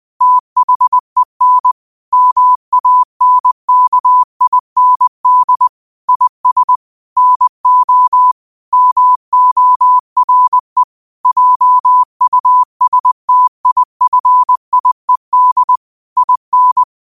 Morse code, heard as a keyed tone, about 20 wpm, 1000 Hz. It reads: THEN MANKIND IS NO MORE JUSTIFIED IN
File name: qotdfiles / 20wpm